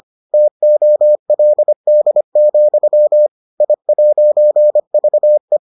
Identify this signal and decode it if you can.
Morse code, heard as a keyed tone, about 25 wpm, 600 Hz. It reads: TOLD, I'VE